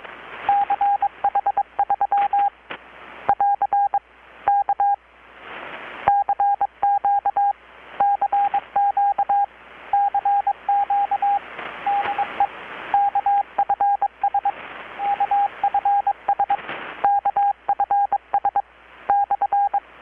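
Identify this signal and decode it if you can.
Morse code, heard as a keyed tone, about 22 wpm, 810 Hz, with radio noise. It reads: CH3 <AR> K CQ CQ CQ DE KFS KFS KFS /
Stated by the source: Morse from KPH's Night of Nights XX maritime radiotelegraphy event, received in Chiapas, Mexico about 3,800 km away on a Kenwood R-600 with a grounded 10 m long-wire antenna.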